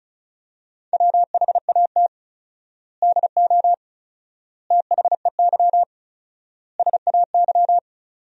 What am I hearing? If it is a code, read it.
Morse code, heard as a keyed tone, about 34 wpm, 700 Hz. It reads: WHAT DO THEY SAY
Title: 4x Word-Spacing - Sentences from Top 200 Words - 35wpm